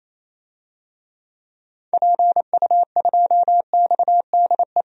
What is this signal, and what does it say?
Morse code, heard as a keyed tone, about 28 wpm, 700 Hz. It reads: PU2XDE
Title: DXCC-100 Call Signs - 6 Characters - 28wpm